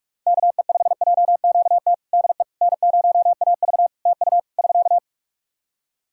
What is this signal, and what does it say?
Morse code, heard as a keyed tone, about 46 wpm, 700 Hz. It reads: KE5JQT DE N0AV TU <SK>
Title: Practice QSO - Exchange 4 - Ending - 45wpm